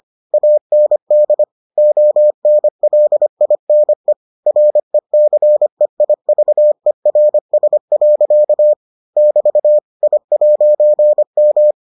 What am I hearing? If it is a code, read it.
Morse code, heard as a keyed tone, about 25 wpm, 600 Hz. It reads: AND ONLINE RECEIVERS. <BT> I'M